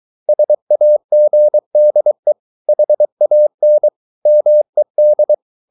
Morse code, heard as a keyed tone, about 23 wpm, 600 Hz. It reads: SAGDE HAN MED
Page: Ferie 23 wpm